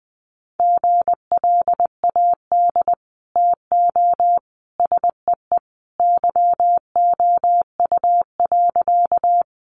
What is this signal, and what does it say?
Morse code, heard as a keyed tone, about 20 wpm, 700 Hz. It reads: GLAD TO SEE YOU.
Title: Sentences from Top 800 Words 20wpm